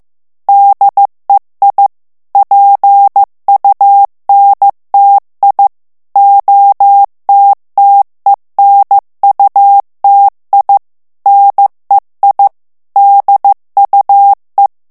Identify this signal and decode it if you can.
Morse code, heard as a keyed tone, about 15 wpm, 800 Hz. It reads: DEI PUNTI OTTENUTI NEI DUE
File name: morse mp3 / ITALIEN texte clair 2 vitesse 14 mots minute